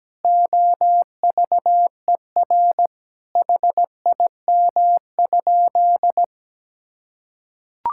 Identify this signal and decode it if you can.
Morse code, heard as a keyed tone, about 17 wpm, 700 Hz. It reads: OVER HIM?
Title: Sentences from Top 1600 Words 17wpm